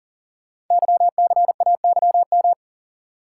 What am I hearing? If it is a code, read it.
Morse code, heard as a keyed tone, about 38 wpm, 690 Hz. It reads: YCAYM
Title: Code Groups - 5 Letters - 40wpm